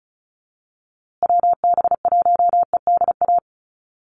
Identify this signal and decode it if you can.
Morse code, heard as a keyed tone, about 34 wpm, 700 Hz. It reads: WB1EDA
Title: USA Call Signs Repeated 2x 35wpm